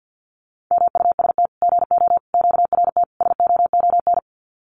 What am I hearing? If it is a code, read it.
Morse code, heard as a keyed tone, about 50 wpm, 700 Hz. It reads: MUST GO QRT SOON
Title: Common QSO Phrases 50wpm